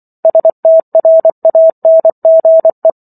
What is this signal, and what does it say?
Morse code, heard as a keyed tone, about 24 wpm, 650 Hz. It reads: STRANGE